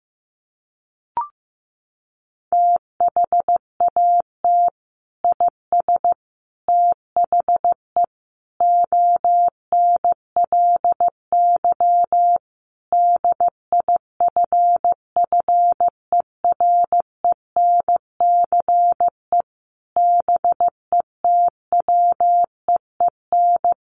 Morse code, heard as a keyed tone, about 14 wpm, 700 Hz. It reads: THAT IS THE ONLY DIFFERENCE BETWEEN
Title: Sentences from Top 800 Words 15wpm